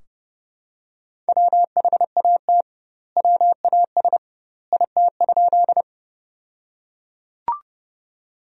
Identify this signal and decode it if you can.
Morse code, heard as a keyed tone, about 30 wpm, 700 Hz. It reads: WHAT WAS IT?
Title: 2x Word-Spacing - Sentences from Top 100 Words - 30wpm